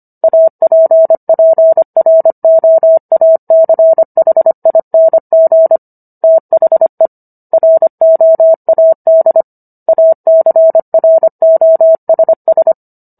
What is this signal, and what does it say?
Morse code, heard as a keyed tone, about 25 wpm, 650 Hz. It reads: APPROACHING THE ROAD ACROSS